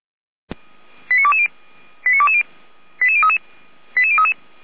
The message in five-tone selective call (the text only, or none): 89220 89220 88033 88033